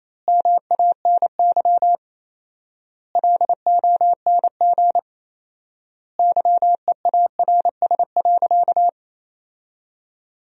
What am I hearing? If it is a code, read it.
Morse code, heard as a keyed tone, about 28 wpm, 700 Hz. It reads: MANY LONG YEARS.
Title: 4x Word-Spacing - Sentences from Top 200 Words - 28wpm